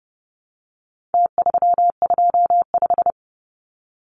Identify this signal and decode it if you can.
Morse code, heard as a keyed tone, about 30 wpm, 700 Hz. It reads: T325
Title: Numbers - Contest Style - 4-Digit 30wpm